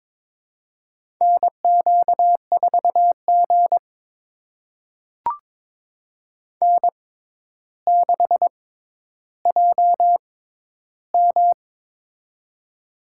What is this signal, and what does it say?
Morse code, heard as a keyed tone, about 22 wpm, spaced as for 11 wpm, 700 Hz. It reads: NQ4G N6JM
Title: USA Call Signs - 4 Characters - 6x Character Spacing - 22wpm